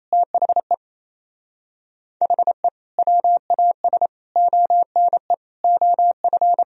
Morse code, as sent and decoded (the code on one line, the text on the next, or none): - .... / .... / .-- .- ... / --- -. / --- ..-.
THE HE WAS ONE OF